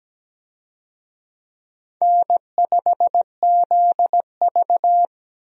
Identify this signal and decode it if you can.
Morse code, heard as a keyed tone, about 17 wpm, 700 Hz. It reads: N5ZV
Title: USA Call Signs - 4 Characters - Repeated 3x - 17wpm